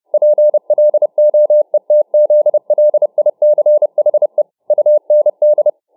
Morse code, heard as a keyed tone, about 29 wpm, 600 Hz. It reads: PLOETZLICHE UND